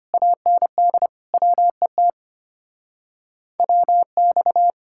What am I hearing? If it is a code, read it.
Morse code, heard as a keyed tone, about 32 wpm, 700 Hz. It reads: AND WET WX